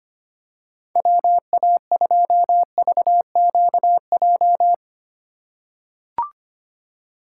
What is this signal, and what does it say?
Morse code, heard as a keyed tone, about 25 wpm, 700 Hz. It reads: WA2VQJ
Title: USA Call Signs - 6 Characters - 3x Character Spacing - 25wpm